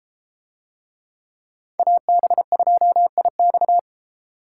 Morse code, heard as a keyed tone, about 32 wpm, 700 Hz. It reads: AB2IX